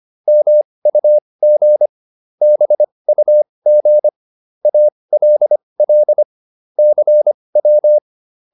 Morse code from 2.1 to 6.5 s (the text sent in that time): BUG ALL